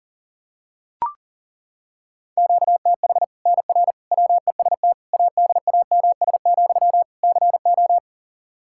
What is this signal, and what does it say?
Morse code, heard as a keyed tone, about 40 wpm, 700 Hz. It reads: QTH NR WEST ADAMS, CO
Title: Practice QSO - Exchange 1 - Beginning - 40wpm